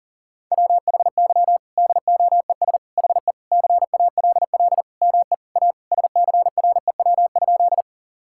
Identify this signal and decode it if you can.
Morse code, heard as a keyed tone, about 40 wpm, 700 Hz. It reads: WHY DOES HE CALL ME A SCREW?
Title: Sentences from Top 1600 Words 40wpm